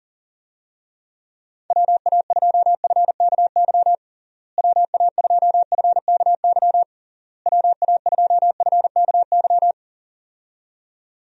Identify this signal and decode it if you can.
Morse code, heard as a keyed tone, about 40 wpm, 700 Hz. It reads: WA2FKY WA2FKY WA2FKY
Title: USA Call Signs - 6 Characters - Repeated 3x - 40wpm